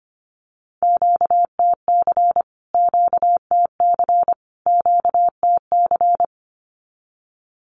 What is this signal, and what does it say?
Morse code, heard as a keyed tone, about 25 wpm, 700 Hz. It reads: QTC QTC QTC
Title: Q-Codes Repeated 3x 25wpm